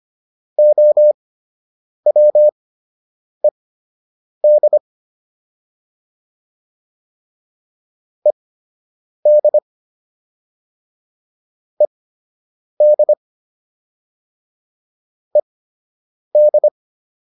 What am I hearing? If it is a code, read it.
Morse code, heard as a keyed tone, about 25 wpm, spaced as for 8 wpm, 600 Hz. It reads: OWED ED ED ED
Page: fundamental-practice-files